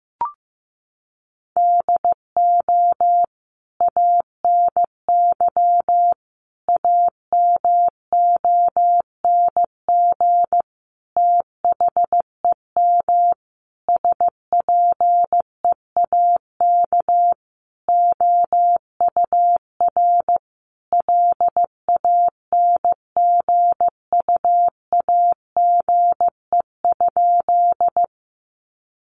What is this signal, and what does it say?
Morse code, heard as a keyed tone, about 15 wpm, 700 Hz. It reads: DO ANY AMONG THEM SPEAK OUR LANGUAGE?
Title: Sentences from Top 1100 Words 15wpm